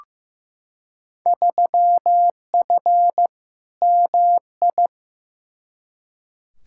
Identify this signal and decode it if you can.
Morse code, heard as a keyed tone, about 15 wpm, 700 Hz. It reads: 3F MI
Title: Field Day Exchanges - 15wpm